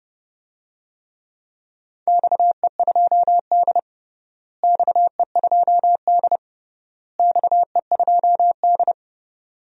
Morse code, heard as a keyed tone, about 30 wpm, 700 Hz. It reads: XE2D XE2D XE2D
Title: DXCC-100 Call Signs - 4 Characters - Repeated 3x - 30wpm